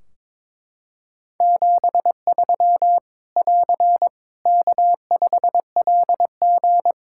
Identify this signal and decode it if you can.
Morse code, heard as a keyed tone, about 22 wpm, 700 Hz. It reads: 73 <AR> K5LG
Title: K1USN SST QSO Practice 22wpm